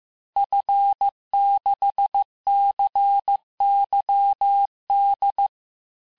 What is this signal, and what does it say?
Morse code, heard as a keyed tone, about 15 wpm, 800 Hz. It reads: F6CYD
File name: cours-du-lundi files / cours-1585-texte-FIN-de-QSO-vitesse-14mpm